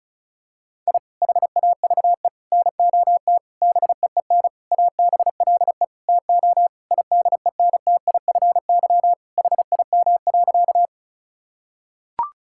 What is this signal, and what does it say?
Morse code, heard as a keyed tone, about 36 wpm, 700 Hz. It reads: I HAVE NOT BEEN ABLE TO IDENTIFY HIM.
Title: Sentences from Top 1000 Words - Review Part 6 - 35wpm